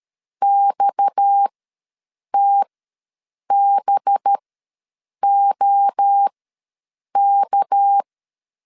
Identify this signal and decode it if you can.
Morse code, heard as a keyed tone, about 13 wpm, spaced as for 7 wpm, 800 Hz. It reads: XTBOK
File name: LEZIONI / 6 PKX0 / 6 PKX ws 12wpm 3106